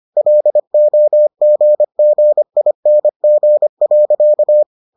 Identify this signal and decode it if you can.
Morse code, heard as a keyed tone, about 25 wpm, 600 Hz. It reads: LOGGING.